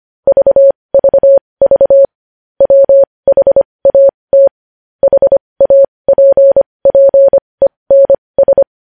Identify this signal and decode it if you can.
Morse code, heard as a keyed tone, about 25 wpm, spaced as for 20 wpm, 550 Hz. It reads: VVV WHAT HAPPENS
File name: podcast / News headlines-1x-25x20